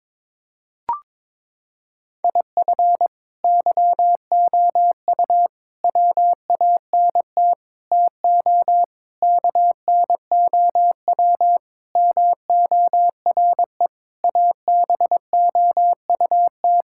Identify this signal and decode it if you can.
Morse code, heard as a keyed tone, about 22 wpm, 700 Hz. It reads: IF YOU WANT TO KNOW MORE ABOUT
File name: practice / 22wpm / Sentences with 8-Word Phrases - 22wpm